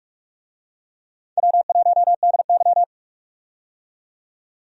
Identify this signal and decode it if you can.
Morse code, heard as a keyed tone, about 46 wpm, 700 Hz. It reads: W1DY
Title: USA Call Signs - 4 Characters - 45wpm